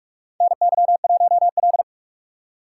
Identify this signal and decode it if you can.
Morse code, heard as a keyed tone, about 44 wpm, 700 Hz. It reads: NY1L
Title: USA Call Signs - 4 Characters - 6x Character Spacing - 45wpm